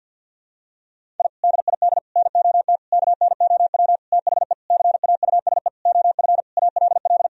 Morse code, heard as a keyed tone, about 52 wpm, 700 Hz. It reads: I DID NOT KNOW THE CAUSE OF ALL
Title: Sentences from Top 1000 Words - Review Part 1 - 50wpm